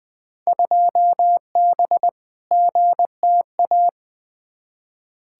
Morse code, heard as a keyed tone, about 20 wpm, 700 Hz. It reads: 2B GTA
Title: Field Day Exchanges Encore - 20wpm